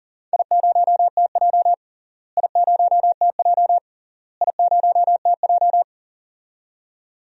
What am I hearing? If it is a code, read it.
Morse code, heard as a keyed tone, about 40 wpm, 700 Hz. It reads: I0TJ I0TJ I0TJ
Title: DXCC-100 Call Signs - 4 Characters - Repeated 3x - 40wpm